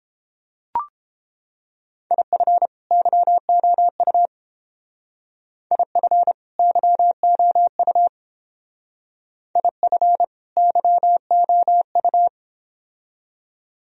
Morse code, heard as a keyed tone, about 28 wpm, 700 Hz. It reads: IF YOU IF YOU IF YOU